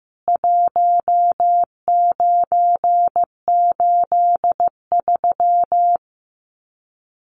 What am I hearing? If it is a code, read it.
Morse code, heard as a keyed tone, about 15 wpm, 700 Hz. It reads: 1983
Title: Common QSO Phrases 15wpm